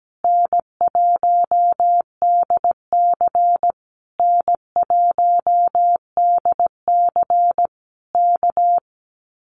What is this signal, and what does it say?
Morse code, heard as a keyed tone, about 17 wpm, 700 Hz. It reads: N1DC N1DC K
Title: Common QSO Phrases Encore 17wpm